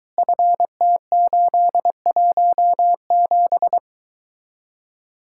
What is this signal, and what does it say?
Morse code, heard as a keyed tone, about 22 wpm, 700 Hz. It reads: FT817